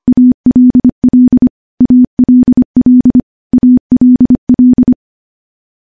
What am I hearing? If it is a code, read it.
Morse code, heard as a keyed tone, about 25 wpm, 250 Hz. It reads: ALL ALL ALL
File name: morse / morse-training-25WPM